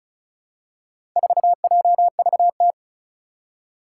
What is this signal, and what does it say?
Morse code, heard as a keyed tone, about 36 wpm, 700 Hz. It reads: 4JVT
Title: Four Challenging Letter-Number Combinations 35wpm